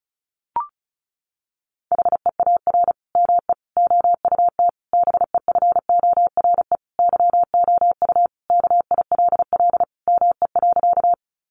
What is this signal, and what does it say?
Morse code, heard as a keyed tone, about 34 wpm, 700 Hz. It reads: HEAR ME OUT BEFORE YOU KILL ME.
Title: Sentences from Top 1400 Words 35wpm